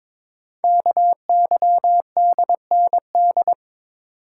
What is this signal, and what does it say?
Morse code, heard as a keyed tone, about 22 wpm, 700 Hz. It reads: KYDND
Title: Code Groups - 5 Letters - 22wpm